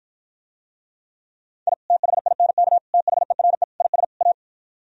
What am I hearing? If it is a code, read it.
Morse code, heard as a keyed tone, about 53 wpm, 700 Hz. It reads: I THINK THERE IS A